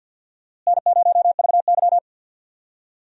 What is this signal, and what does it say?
Morse code, heard as a keyed tone, about 50 wpm, 700 Hz. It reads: N0VY
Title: USA Call Signs - 4 Characters - Repeated 2x - 50wpm